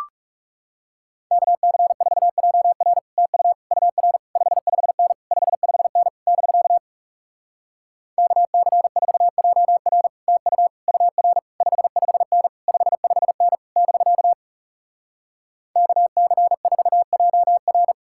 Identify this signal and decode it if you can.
Morse code, heard as a keyed tone, about 38 wpm, 700 Hz. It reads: KC4JR TU UR 55N 55N <BK> KC4JR TU UR 55N 55N <BK> KC4JR